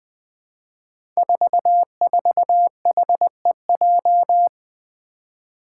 Morse code, heard as a keyed tone, about 20 wpm, 700 Hz. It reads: 44HEJ
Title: Five Challenging Letter-Number Combinations 20wpm